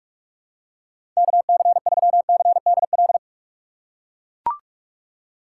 Morse code, heard as a keyed tone, about 43 wpm, 700 Hz. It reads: KC3CDL
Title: USA Call Signs - 3x Character Spacing - 45wpm